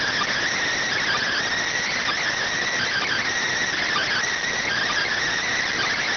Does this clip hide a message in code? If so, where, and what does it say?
Yes: Emergency Alert System (SAME) tones from 1.0 to 5.1 s, heard as harsh buzzing bursts.